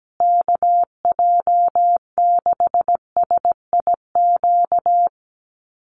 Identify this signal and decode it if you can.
Morse code, heard as a keyed tone, about 17 wpm, 700 Hz. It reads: KJ6SIQ
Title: USA Call Signs Repeated 2x 17wpm